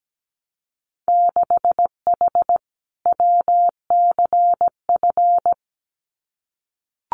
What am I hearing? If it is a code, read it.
Morse code, heard as a keyed tone, about 17 wpm, 700 Hz. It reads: 6H WCF
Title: Winter Field Day Exchanges Encore 17wpm